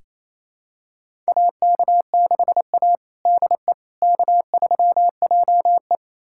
Morse code, heard as a keyed tone, about 28 wpm, 700 Hz. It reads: AK6A DE K3JE
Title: USA Callsign DE Callsign - 4 Characters - 28wpm